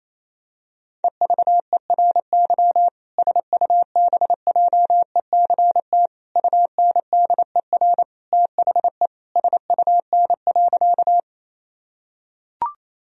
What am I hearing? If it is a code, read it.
Morse code, heard as a keyed tone, about 28 wpm, 700 Hz. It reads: EVERY SUBJECT UNDER THE SUN.
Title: Sentences from Top 1000 Words - Review Part 1 - 28wpm